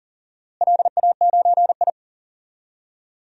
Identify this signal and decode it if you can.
Morse code, heard as a keyed tone, about 39 wpm, 700 Hz. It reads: LA9I
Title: DXCC-100 Call Signs - 4 Characters - Repeated 3x - 40wpm